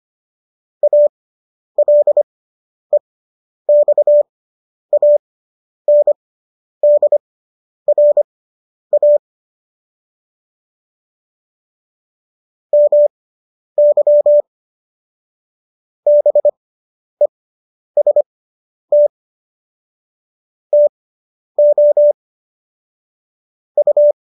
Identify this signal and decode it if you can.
Morse code, heard as a keyed tone, about 25 wpm, spaced as for 10 wpm, 590 Hz. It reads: ALEXANDRA MY BEST TO U
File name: qso219 10